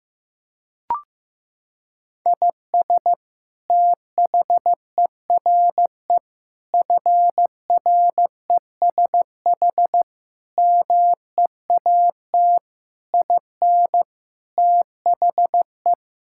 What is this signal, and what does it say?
Morse code, heard as a keyed tone, about 15 wpm, 700 Hz. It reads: IS THERE FRESH MEAT IN THE
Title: Sentences from Top 1600 Words 15wpm